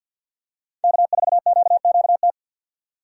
Morse code, heard as a keyed tone, about 49 wpm, 700 Hz. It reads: K4QQT
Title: USA Call Signs - 5 Characters - 4x Character Spacing - 50wpm